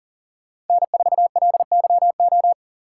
Morse code, heard as a keyed tone, about 50 wpm, 700 Hz. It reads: N4LYO
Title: Speed Racing - USA 1x3 Call Signs 40wpm